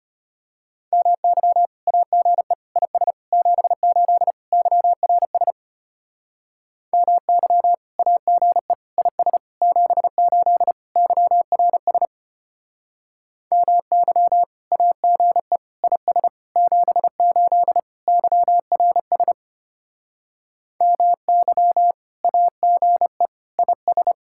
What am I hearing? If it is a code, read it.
Morse code, heard as a keyed tone, about 33 wpm, 700 Hz. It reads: MY AGE IS 78 YRS MY AGE IS 78 YRS MY AGE IS 78 YRS MY AGE IS